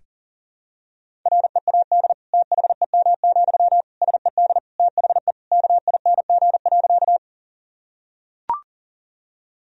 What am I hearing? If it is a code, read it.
Morse code, heard as a keyed tone, about 40 wpm, 700 Hz. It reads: READ THEM, SED THE KING.
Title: Sentences from Top 400 Words and Abbreviations 40wpm